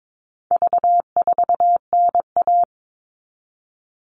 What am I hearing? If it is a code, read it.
Morse code, heard as a keyed tone, about 22 wpm, 700 Hz. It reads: V4NA
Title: All Cut Numbers - 4-Digit 22wpm